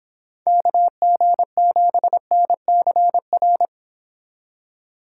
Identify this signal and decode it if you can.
Morse code, heard as a keyed tone, about 29 wpm, 700 Hz. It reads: KG7NCR